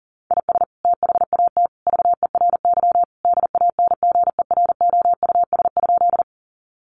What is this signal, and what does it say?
Morse code, heard as a keyed tone, about 40 wpm, 700 Hz. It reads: IS THAT VERY DANGEROUS?
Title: Sentences from Top 1200 Words 40wpm